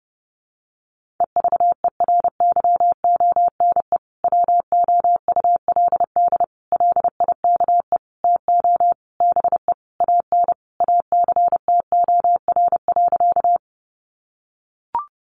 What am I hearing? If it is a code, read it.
Morse code, heard as a keyed tone, about 30 wpm, 700 Hz. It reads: EVERYONE WOULD LIKE TO BE AN ACTOR.
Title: Sentences from Top 1500 Words 30wpm